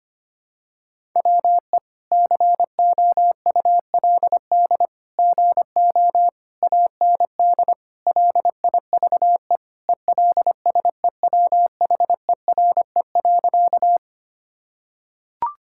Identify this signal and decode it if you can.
Morse code, heard as a keyed tone, about 25 wpm, 700 Hz. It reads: WE COULD GO AND LIVE ELSEWHERE.
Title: Sentences from Top 1600 Words 25wpm